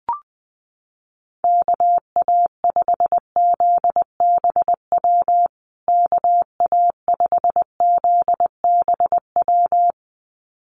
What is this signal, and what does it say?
Morse code, heard as a keyed tone, about 20 wpm, 700 Hz. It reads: KA5ZBW KA5ZBW
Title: USA Call Signs Repeated 2x 20wpm